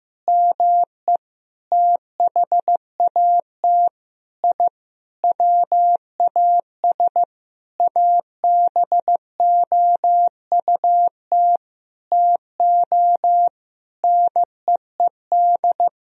Morse code, heard as a keyed tone, about 15 wpm, 700 Hz. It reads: ME THAT I WAS ABOUT TO NEED
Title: Sentences from Top 1000 Words - Review Part 1 - 15wpm